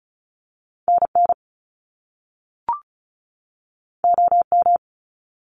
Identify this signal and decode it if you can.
Morse code, heard as a keyed tone, about 36 wpm, 700 Hz. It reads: NN OM
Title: Top Two Letter Combinations in Words 35wpm